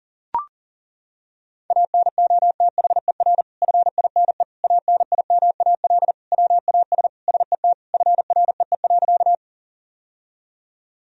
Morse code, heard as a keyed tone, about 40 wpm, 700 Hz. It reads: ANOTHER FINE ANIMAL WAS SET FREE.
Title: Sentences from Top 1000 Words - Review Part 1 - 40wpm